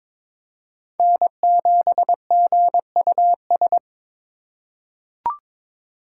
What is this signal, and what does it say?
Morse code, heard as a keyed tone, about 22 wpm, 700 Hz. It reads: N7GUS
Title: USA Call Signs - 2x Character Spacing - 22wpm